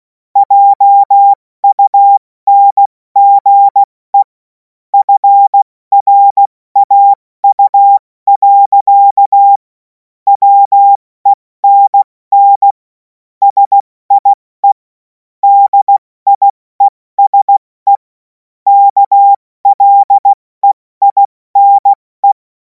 Morse code, heard as a keyed tone, about 16 wpm, 800 Hz. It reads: JUNGE FRAU. WENN SIE DIESE KLEINE
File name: Morsekurs / Witze / Witz 03. Februar 2026 70 F80